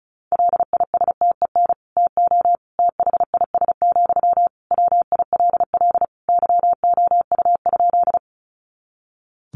Morse code, heard as a keyed tone, about 34 wpm, 700 Hz. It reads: LISTEN TO THIS, WILL YOU?
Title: Sentences from Top 400 Words 35wpm